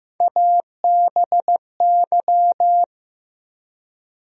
Morse code, heard as a keyed tone, about 15 wpm, 700 Hz. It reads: ABY
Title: USA Call Sign Postfixes - 3 Letters 15wpm